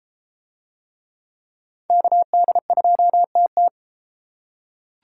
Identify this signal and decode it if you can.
Morse code, heard as a keyed tone, about 31 wpm, 700 Hz. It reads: KD2TT